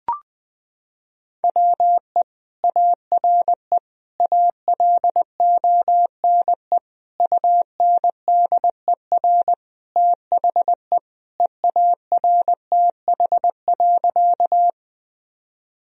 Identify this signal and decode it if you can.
Morse code, heard as a keyed tone, about 20 wpm, 700 Hz. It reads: WE ARE ALONE UNDER THE EARTH.
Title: Sentences from Top 1000 Words - Review Part 1 - 20wpm